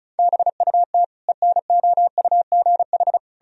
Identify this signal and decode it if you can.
Morse code, heard as a keyed tone, about 36 wpm, 700 Hz. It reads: BUT ENOUGH
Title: Sentences from Top 1000 Words - Review Part 2 - 35wpm